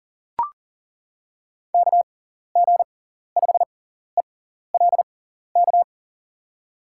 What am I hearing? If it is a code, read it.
Morse code, heard as a keyed tone, about 39 wpm, spaced as for 14 wpm, 700 Hz. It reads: KG5ELK
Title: USA Call Signs - 6 Characters - 6x Character Spacing - 40wpm